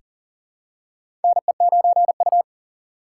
Morse code, heard as a keyed tone, about 40 wpm, 700 Hz. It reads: NE9U
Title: K1USN SST QSO Practice - 40wpm